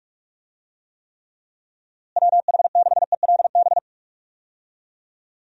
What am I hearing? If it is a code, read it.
Morse code, heard as a keyed tone, about 41 wpm, 700 Hz. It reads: WH6ELB
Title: USA Call Signs - 6 Characters - Repeated 3x - 45wpm